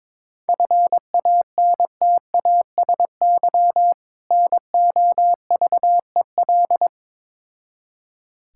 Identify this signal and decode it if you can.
Morse code, heard as a keyed tone, about 22 wpm, 700 Hz. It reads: FANTASY NOVEL
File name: Difficult Words with Hints - 22wpm